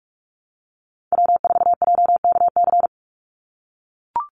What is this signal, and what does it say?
Morse code, heard as a keyed tone, about 44 wpm, 700 Hz. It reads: W4JKC
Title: NA SOTA Chaser Callsigns 45wpm